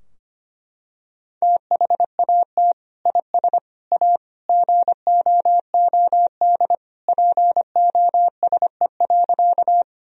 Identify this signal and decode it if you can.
Morse code, heard as a keyed tone, about 25 wpm, 700 Hz. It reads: THAT IS A GOOD POSE.
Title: Sentences from Top 1000 Words - Review Part 2 - 25wpm